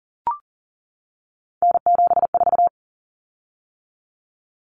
Morse code, heard as a keyed tone, about 40 wpm, 700 Hz. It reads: N74
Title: Common Cut Numbers - 3-Digit 40wpm